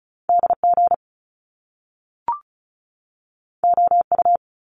Morse code, heard as a keyed tone, about 36 wpm, 700 Hz. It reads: DG OU
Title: Top Two Letter Combinations in Words 35wpm